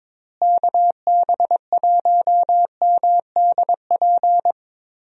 Morse code, heard as a keyed tone, about 22 wpm, 700 Hz. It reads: KB1MDP
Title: USA Call Signs - 6 Characters - Repeated 2x - 22wpm